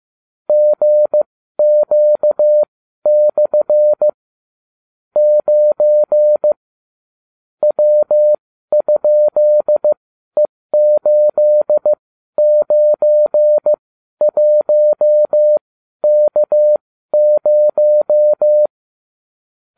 Morse code, heard as a keyed tone, about 15 wpm, spaced as for 12 wpm, 600 Hz. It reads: GQ/ 9 W?E891K0